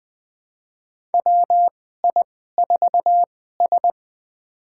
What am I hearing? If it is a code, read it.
Morse code, heard as a keyed tone, about 20 wpm, spaced as for 15 wpm, 700 Hz. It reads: WI4S
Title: USA Call Signs - 4 Characters - 2x Character Spacing - 20wpm